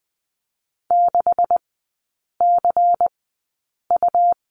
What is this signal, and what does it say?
Morse code, heard as a keyed tone, about 20 wpm, spaced as for 8 wpm, 700 Hz. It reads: 6CU
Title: 138 Lesson 23e - Letter f - Three Characters 20wpm